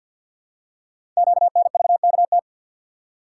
Morse code, heard as a keyed tone, about 53 wpm, 700 Hz. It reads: QNVKT
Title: Code Groups - 5 Letter:Numbers - 50wpm